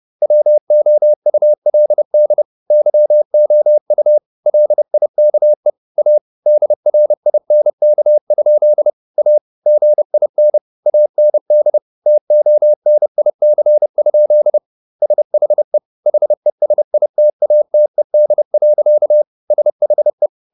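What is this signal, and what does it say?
Morse code, heard as a keyed tone, about 30 wpm, 600 Hz. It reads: WOULD YOU LIKE A DRINK? A GIN AND TONIC? SHE HESITATED. SHE
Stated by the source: Morse code at 30 wpm, the novel Never by Ken Follett in English, from CW med Gnister.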